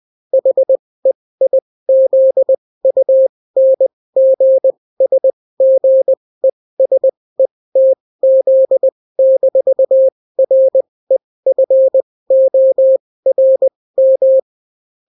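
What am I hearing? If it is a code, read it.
Morse code, heard as a keyed tone, about 20 wpm, spaced as for 16 wpm, 530 Hz. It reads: HEIZUNGSGESETZ-REFORM
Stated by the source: Daily Morse weather and top stories at 16 wpm, clean without QRN.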